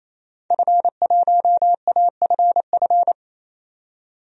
Morse code, heard as a keyed tone, about 28 wpm, 700 Hz. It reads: F1AFF
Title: DXCC-100 Call Signs - 5 Characters - 28wpm